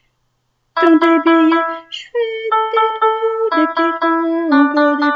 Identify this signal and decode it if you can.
There is no code.